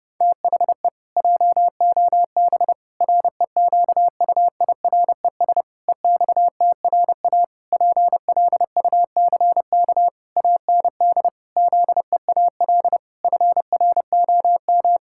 Morse code, heard as a keyed tone, about 32 wpm, 700 Hz. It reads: THE JOB REQUIRES EXTRA PLUCK AND ZEAL FROM
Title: Pangram Challenge v2 - 30wpm